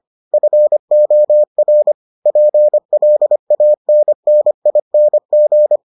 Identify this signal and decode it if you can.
Morse code, heard as a keyed tone, about 25 wpm, 600 Hz. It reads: FOR PLANNING